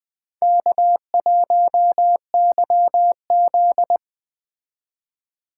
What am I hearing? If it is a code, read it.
Morse code, heard as a keyed tone, about 20 wpm, 700 Hz. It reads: K1YZ